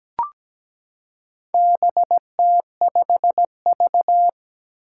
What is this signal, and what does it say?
Morse code, heard as a keyed tone, about 17 wpm, 700 Hz. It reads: BT5V
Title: Four Challenging Letter-Number Combinations 17wpm